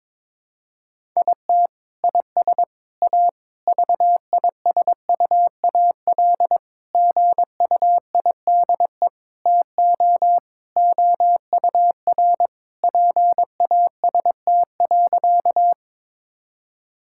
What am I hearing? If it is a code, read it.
Morse code, heard as a keyed tone, about 21 wpm, 700 Hz. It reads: IT IS A VISUAL GUIDE TO OUR PAST.
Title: Sentences from Top 1000 Words - Review Part 6 - 22wpm